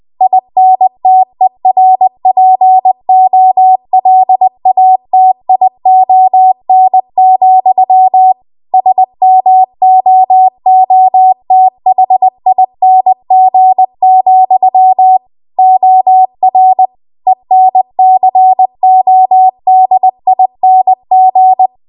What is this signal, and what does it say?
Morse code, heard as a keyed tone, about 20 wpm, 750 Hz. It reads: INTERPOLATION, SMOOTHING, OR ENCODING
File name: Archive / 20 WPM / 20WPM